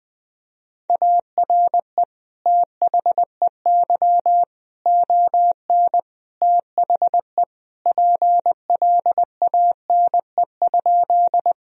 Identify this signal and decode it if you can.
Morse code, heard as a keyed tone, about 21 wpm, 700 Hz. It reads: ARE THEY ON THE PLANE?
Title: Sentences from Top 1000 Words - Review Part 3 - 20wpm